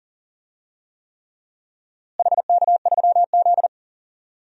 Morse code, heard as a keyed tone, about 38 wpm, 700 Hz. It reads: HK3Z
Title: DXCC-100 Call Signs - 4 Characters - Repeated 3x - 40wpm